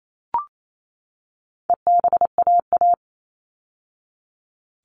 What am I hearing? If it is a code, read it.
Morse code, heard as a keyed tone, about 28 wpm, 700 Hz. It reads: EBAA
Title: All Cut Numbers - 4-Digit 28wpm